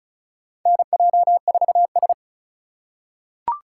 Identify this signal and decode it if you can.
Morse code, heard as a keyed tone, about 36 wpm, 700 Hz. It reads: NJ4S
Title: USA Call Signs - 3x Character Spacing - 35wpm